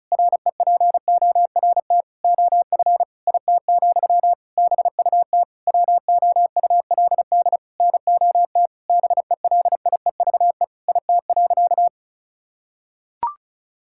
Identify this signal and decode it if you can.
Morse code, heard as a keyed tone, about 36 wpm, 700 Hz. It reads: REPORT OF IT, BUT WOULD NOT BELIEVE IT.
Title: Sentences from Top 1000 Words - Review Part 6 - 35wpm